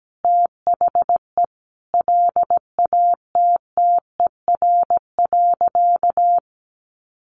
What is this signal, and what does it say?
Morse code, heard as a keyed tone, about 17 wpm, 700 Hz. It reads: THE LATTER.
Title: A Sound Of Thunder 17wpm